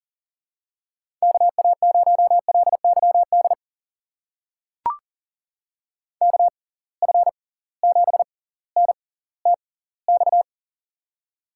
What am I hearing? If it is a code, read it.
Morse code, heard as a keyed tone, about 40 wpm, spaced as for 20 wpm, 700 Hz. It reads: KA0LYD KF7NTX
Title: USA Call Signs - 6 Characters - 6x Character Spacing - 40wpm